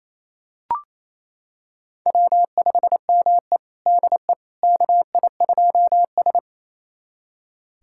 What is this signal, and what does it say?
Morse code, heard as a keyed tone, about 28 wpm, 690 Hz. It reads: W5ME DE KI2S
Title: USA Callsign DE Callsign - 4 Characters - 28wpm